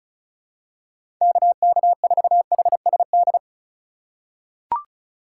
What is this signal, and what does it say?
Morse code, heard as a keyed tone, about 34 wpm, 700 Hz. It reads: KK4HSD
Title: USA Call Signs - 2x Character Spacing - 35wpm